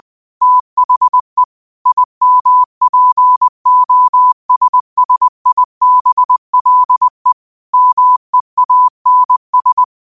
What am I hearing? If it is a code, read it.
Morse code, heard as a keyed tone, about 20 wpm, 1000 Hz. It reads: THE IMPOSSIBLE MEANS
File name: qotdfiles / 20wpm